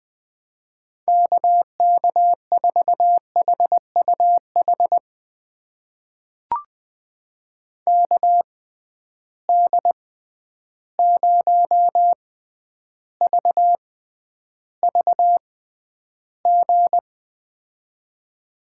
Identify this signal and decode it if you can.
Morse code, heard as a keyed tone, about 20 wpm, spaced as for 10 wpm, 700 Hz. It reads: KK4HUH KD0VVG
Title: USA Call Signs - 6 Characters - 6x Character Spacing - 20wpm